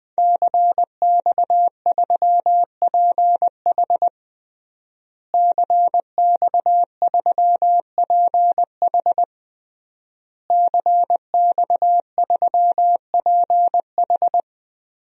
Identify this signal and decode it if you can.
Morse code, heard as a keyed tone, about 20 wpm, 700 Hz. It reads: CX3PH CX3PH CX3PH
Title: DXCC-100 Call Signs - 5 Characters - Repeated 3x - 20wpm